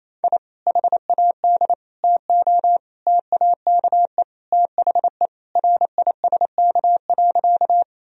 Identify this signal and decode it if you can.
Morse code, heard as a keyed tone, about 28 wpm, 700 Hz. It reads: I HAD TO TAKE THE RISK.
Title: Sentences from Top 1000 Words - Review Part 4 - 28wpm